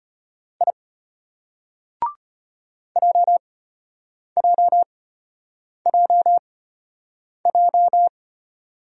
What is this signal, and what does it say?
Morse code, heard as a keyed tone, about 33 wpm, spaced as for 8 wpm, 700 Hz. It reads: I JJJJ